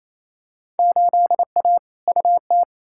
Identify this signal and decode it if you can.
Morse code, heard as a keyed tone, about 28 wpm, 700 Hz. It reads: 8A UT
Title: Field Day Exchanges Encore - 28wpm